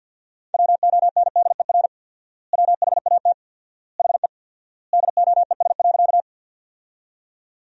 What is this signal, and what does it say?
Morse code, heard as a keyed tone, about 52 wpm, 700 Hz. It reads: WONDER WHAT HE DOES.
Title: 4x Word-Spacing - Sentences from Top 500 Words - 50wpm